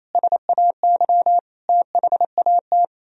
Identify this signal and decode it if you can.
Morse code, heard as a keyed tone, about 34 wpm, 700 Hz. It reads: SAY THAT
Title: Speed-Racing - Sets of 3 Words 28wpm